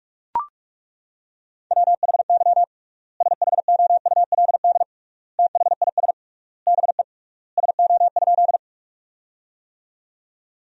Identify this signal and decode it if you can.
Morse code, heard as a keyed tone, about 42 wpm, 700 Hz. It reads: WHY SHOULD THIS BE SO?
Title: 3x Word-Spacing - Sentences from Top 300 Words - 45wpm